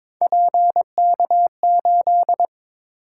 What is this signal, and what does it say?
Morse code, heard as a keyed tone, about 22 wpm, 700 Hz. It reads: PK8
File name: practice / 22wpm / Three Letter-Number Combinations - 22wpm